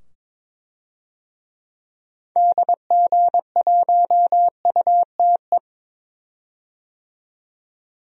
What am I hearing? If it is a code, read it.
Morse code, heard as a keyed tone, about 21 wpm, 700 Hz. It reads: DG1UTE
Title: DXCC-100 Call Signs - 6 Characters - 22wpm